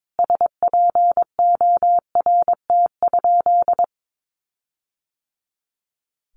Morse code, heard as a keyed tone, about 22 wpm, 700 Hz. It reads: SPORT?
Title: Sentences from Top 1500 Words 22wpm